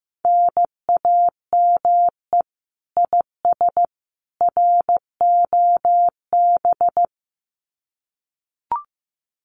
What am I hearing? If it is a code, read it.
Morse code, heard as a keyed tone, about 15 wpm, 700 Hz. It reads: NAME IS ROB
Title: Common QSO Phrases 15wpm